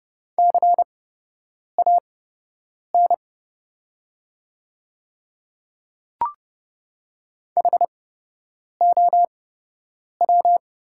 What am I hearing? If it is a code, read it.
Morse code, heard as a keyed tone, about 30 wpm, spaced as for 8 wpm, 700 Hz. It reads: CAN HOW
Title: 8x Character Spacing - Top 100 Words - 30wpm